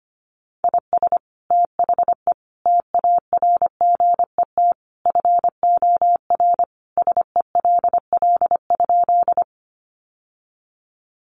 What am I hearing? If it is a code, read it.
Morse code, heard as a keyed tone, about 25 wpm, 700 Hz. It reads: IS THE TARGET FOR SELL?
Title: Sentences from Top 1200 Words 25wpm